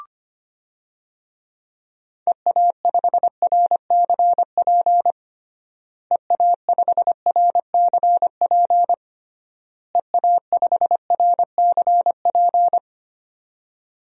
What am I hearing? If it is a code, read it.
Morse code, heard as a keyed tone, about 25 wpm, 700 Hz. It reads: EA5RCP EA5RCP EA5RCP
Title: DXCC-100 Call Signs - 6 Characters - Repeated 3x - 25wpm